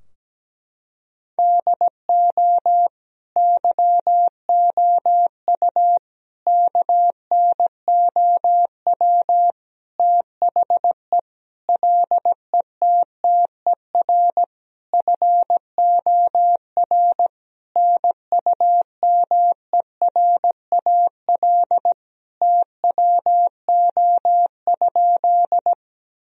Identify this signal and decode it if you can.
Morse code, heard as a keyed tone, about 17 wpm, 700 Hz. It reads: DO YOU KNOW THE LETTER FOR NUMERAL TWO?
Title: Sentences from Top 1000 Words - Review Part 2 - 17wpm